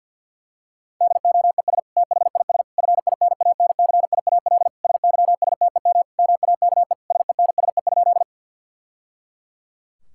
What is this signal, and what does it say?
Morse code, heard as a keyed tone, about 50 wpm, 700 Hz. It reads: DOES THIS FINANCIAL SYSTEM MAKE SENSE?
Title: Sentences from Top 1000 Words - Review Part 5 - 50wpm